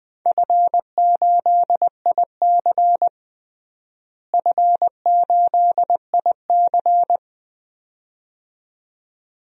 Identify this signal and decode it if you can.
Morse code, heard as a keyed tone, about 20 wpm, 700 Hz. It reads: F8IC F8IC
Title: DXCC-100 Call Signs - 4 Characters - Repeated 2x - 20wpm